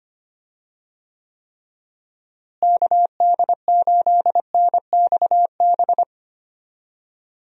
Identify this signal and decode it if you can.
Morse code, heard as a keyed tone, about 25 wpm, 700 Hz. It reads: KD8NXB
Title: USA Call Signs - 6 Characters - Repeated 3x - 25wpm